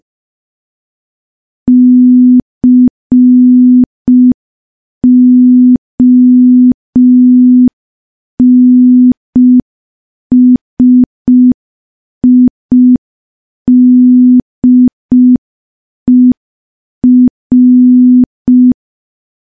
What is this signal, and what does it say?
Morse code, heard as a keyed tone, about 5 wpm, 250 Hz. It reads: CONSIDER
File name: morse / morse-training-05WPM